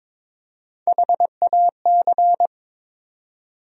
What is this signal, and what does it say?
Morse code, heard as a keyed tone, about 22 wpm, 700 Hz. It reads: HAC
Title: USA Call Sign Postfixes - 3 Letters 22wpm